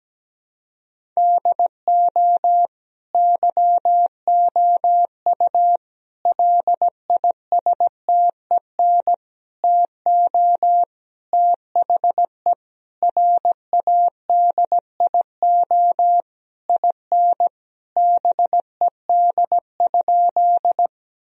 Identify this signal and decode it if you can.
Morse code, heard as a keyed tone, about 17 wpm, 700 Hz. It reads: DO YOU LISTEN TO THE RADIO IN BED?
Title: Sentences from Top 1000 Words - Review Part 3 - 17wpm